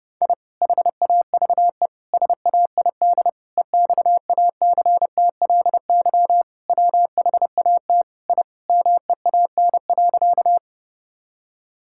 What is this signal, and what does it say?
Morse code, heard as a keyed tone, about 32 wpm, 700 Hz. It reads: I HAVE SAID EXACTLY WHAT I MEAN.
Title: Sentences from Top 1000 Words - Review Part 6 - 30wpm